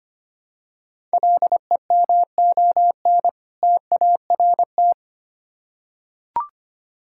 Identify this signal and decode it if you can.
Morse code, heard as a keyed tone, about 25 wpm, 700 Hz. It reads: LEMON TART
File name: practice / 25wpm / Difficult Words with Hints - 25wpm